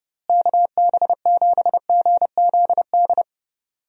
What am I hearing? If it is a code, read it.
Morse code, heard as a keyed tone, about 32 wpm, 700 Hz. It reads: KB7GZD
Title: USA Call Signs - 6 Characters - Repeated 2x - 30wpm